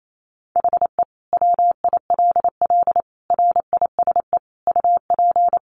Morse code, heard as a keyed tone, about 28 wpm, 700 Hz. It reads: HE WILL RISE UP
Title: Sentences from Top 900 Words 28wpm